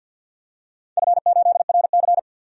Morse code, heard as a keyed tone, about 51 wpm, 700 Hz. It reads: F8RC
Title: DXCC-100 Call Signs - 4 Characters - 50wpm